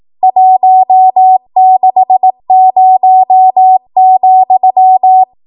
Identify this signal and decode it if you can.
Morse code, heard as a keyed tone, about 18 wpm, 750 Hz. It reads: 160,
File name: Morse / 18WPM